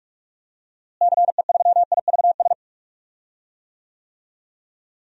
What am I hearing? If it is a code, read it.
Morse code, heard as a keyed tone, about 47 wpm, 690 Hz. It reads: CE3IVS
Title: DXCC-100 Call Signs - 6 Characters - Repeated 2x - 45wpm